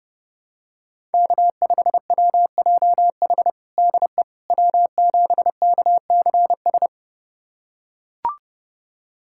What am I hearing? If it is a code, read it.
Morse code, heard as a keyed tone, about 30 wpm, 700 Hz. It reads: K5WJH DE W7KCS
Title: USA Callsign DE Callsign - 5 Characters - 30wpm